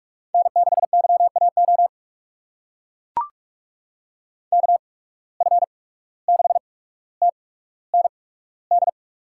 Morse code, heard as a keyed tone, about 43 wpm, spaced as for 18 wpm, 700 Hz. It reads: N6YAO KF6TND
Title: USA Call Signs - 8x Character Spacing - 45wpm